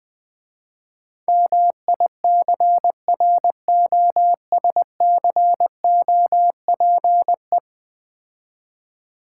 Morse code, heard as a keyed tone, about 20 wpm, 700 Hz. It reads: MICROSCOPE
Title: Easy Words - Speed-Racing with Hints - 20wpm